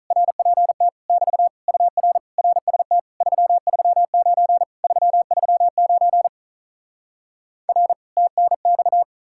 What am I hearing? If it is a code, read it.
Morse code, heard as a keyed tone, about 48 wpm, 700 Hz. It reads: RPT <BT> UR RST 339 339 R TNX